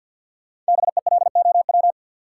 Morse code, heard as a keyed tone, about 51 wpm, 700 Hz. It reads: BELOW
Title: Sentences from Top 1000 Words - Review Part 5 - New Word Drill - 50wpm